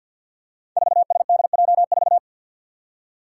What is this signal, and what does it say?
Morse code, heard as a keyed tone, about 52 wpm, 700 Hz. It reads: 4SDJ4
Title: Five Challenging Letter-Number Combinations 50wpm